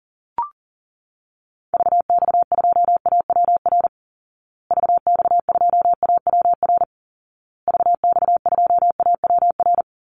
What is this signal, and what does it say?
Morse code, heard as a keyed tone, about 40 wpm, 700 Hz. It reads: VX2AWR VX2AWR VX2AWR
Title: Call Signs Repeated 2x - 40wpm